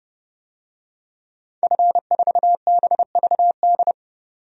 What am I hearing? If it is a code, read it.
Morse code, heard as a keyed tone, about 32 wpm, 700 Hz. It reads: F4BVD
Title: DXCC-100 Call Signs - 5 Characters - 30wpm